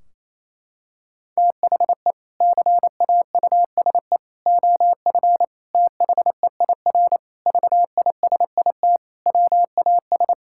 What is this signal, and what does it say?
Morse code, heard as a keyed tone, about 28 wpm, 700 Hz. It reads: THE CAUSE OF THEIR VISIT WAS
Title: Sentences from Top 1600 Words 28wpm